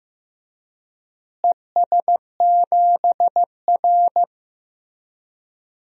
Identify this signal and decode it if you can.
Morse code, heard as a keyed tone, about 15 wpm, 700 Hz. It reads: ES7R
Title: DXCC-100 Call Signs - 4 Characters - Repeated 2x - 15wpm